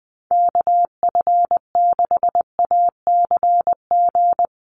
Morse code, heard as a keyed tone, about 20 wpm, 700 Hz. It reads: KF6ACG
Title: USA Call Signs 20wpm